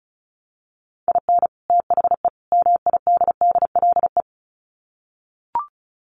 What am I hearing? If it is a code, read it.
Morse code, heard as a keyed tone, about 34 wpm, 700 Hz. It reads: IN THE MIDDLE
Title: Sets of 3 Words 35wpm